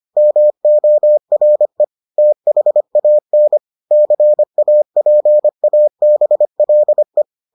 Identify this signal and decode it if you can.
Morse code, heard as a keyed tone, about 25 wpm, 600 Hz. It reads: MORE THAN CAPABLE